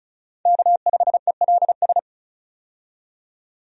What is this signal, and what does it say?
Morse code, heard as a keyed tone, about 36 wpm, 700 Hz. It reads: K5ELS
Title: USA Call Signs - 5 Characters - 35wpm